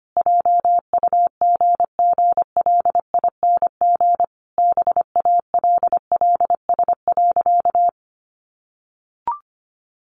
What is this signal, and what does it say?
Morse code, heard as a keyed tone, about 25 wpm, 700 Hz. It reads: JUGGLING BALLS.
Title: Easy Phrases 25wpm